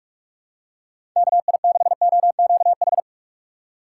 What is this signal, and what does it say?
Morse code, heard as a keyed tone, about 44 wpm, 700 Hz. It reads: KI6OQH
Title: USA Call Signs - 6 Characters - 3x Character Spacing - 45wpm